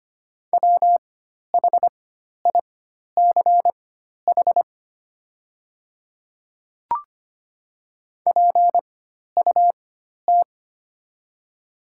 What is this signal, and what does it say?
Morse code, heard as a keyed tone, about 25 wpm, spaced as for 12 wpm, 690 Hz. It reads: WHICH PUT
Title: 4x Character Spacing - Top 300 Words - 25wpm